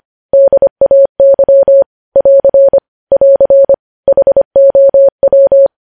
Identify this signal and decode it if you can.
Morse code, heard as a keyed tone, about 25 wpm, 550 Hz. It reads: DAY <AR> <AR> HOW
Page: News Headlines in Morse Code at 25 WPM